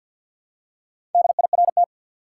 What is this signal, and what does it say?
Morse code, heard as a keyed tone, about 51 wpm, 700 Hz. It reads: DIRT
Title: Four Letter Words - 50wpm